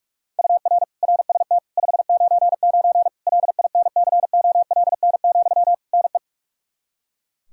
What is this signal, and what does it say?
Morse code, heard as a keyed tone, about 41 wpm, 700 Hz. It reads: UR RST 599 LINCOLN, NE
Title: Practice QSO - Exchange 1 - Beginning - 45wpm